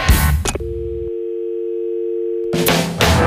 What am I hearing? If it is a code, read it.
Dial tone (UK), 350 + 450 Hz.